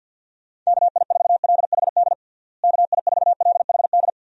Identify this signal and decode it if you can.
Morse code, heard as a keyed tone, about 51 wpm, 700 Hz. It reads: KI4LHD KI4LHD
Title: USA Call Signs - 6 Characters - Repeated 2x - 50wpm